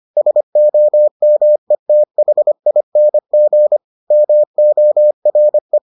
Morse code, heard as a keyed tone, about 25 wpm, 600 Hz. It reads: SOMETHING MORE